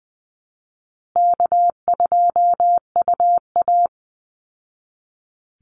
Morse code, heard as a keyed tone, about 20 wpm, 700 Hz. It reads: K2UA
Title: Call Signs Repeated 2x - 20wpm